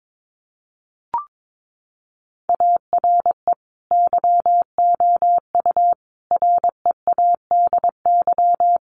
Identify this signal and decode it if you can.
Morse code, heard as a keyed tone, about 22 wpm, 700 Hz. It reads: ARE YOU READY
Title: Sentences from Top 400 Words 22wpm